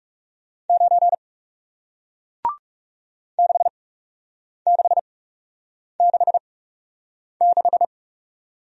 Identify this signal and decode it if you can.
Morse code, heard as a keyed tone, about 40 wpm, spaced as for 8 wpm, 700 Hz. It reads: 9 6666